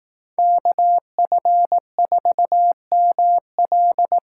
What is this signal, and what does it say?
Morse code, heard as a keyed tone, about 18 wpm, 700 Hz. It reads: KF4ML
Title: Speed Racing - USA 2x2 Call Signs 15wpm